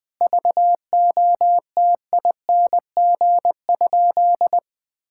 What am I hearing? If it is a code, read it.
Morse code, heard as a keyed tone, about 20 wpm, 700 Hz. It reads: VOTING?
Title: Sentences with 9-Word Phrases - Spoken Beginning - 20wpm